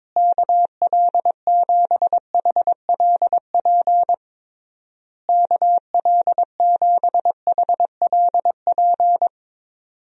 Morse code, heard as a keyed tone, about 22 wpm, 700 Hz. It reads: KL7HLP KL7HLP
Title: DXCC-100 Call Signs - 6 Characters - Repeated 3x - 22wpm